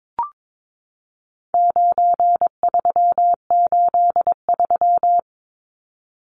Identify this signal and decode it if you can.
Morse code, heard as a keyed tone, about 22 wpm, 700 Hz. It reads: 9383
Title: Numbers - 4-Digit 22wpm